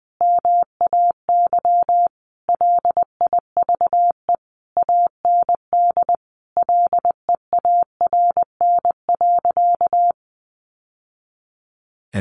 Morse code, heard as a keyed tone, about 20 wpm, 700 Hz. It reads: MAY LIVE AND LEARN.
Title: Sentences from Top 800 Words 20wpm